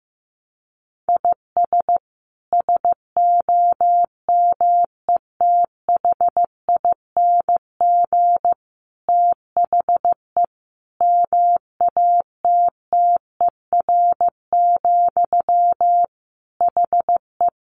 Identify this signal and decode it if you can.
Morse code, heard as a keyed tone, about 15 wpm, 700 Hz. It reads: IS SOMETHING THE MATTER, HE
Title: Sentences from Top 700 Words 15wpm